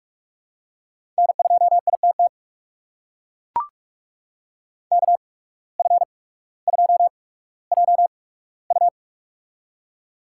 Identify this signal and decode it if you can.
Morse code, heard as a keyed tone, about 44 wpm, spaced as for 19 wpm, 690 Hz. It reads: N2ITT KF2JU
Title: USA Call Signs - 5 Characters - 8x Character Spacing - 45wpm